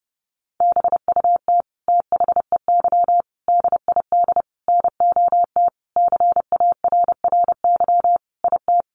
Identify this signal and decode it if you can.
Morse code, heard as a keyed tone, about 30 wpm, 700 Hz. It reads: BUT THEY DID NOT CARRY IT
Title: Sentences from Top 1400 Words 30wpm